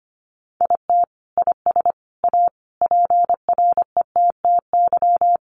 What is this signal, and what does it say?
Morse code, heard as a keyed tone, about 25 wpm, 700 Hz. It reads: IT IS A PRETTY
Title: Sentences from Top 1000 Words 25wpm